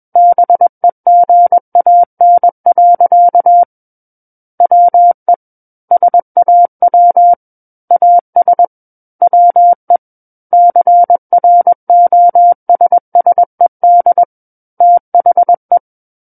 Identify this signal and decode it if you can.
Morse code, heard as a keyed tone, about 21 wpm, 700 Hz. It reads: BEGAN. WE SAW AS WE CROSSED THE
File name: Book One / 12-Chapter 12 - 21 WPM